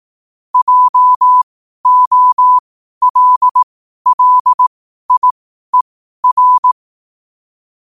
Morse code, heard as a keyed tone, about 18 wpm, spaced as for 12 wpm, 1000 Hz. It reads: JOLLIER
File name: 12wpm